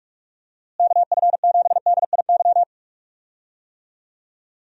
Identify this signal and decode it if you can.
Morse code, heard as a keyed tone, about 46 wpm, 700 Hz. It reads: KF7DIY
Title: USA Call Signs - 6 Characters - Repeated 2x - 45wpm